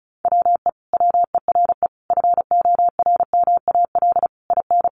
Morse code, heard as a keyed tone, about 34 wpm, 700 Hz. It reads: WE WERE FORMAL IN
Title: Sentences from Top 1300 Words 35wpm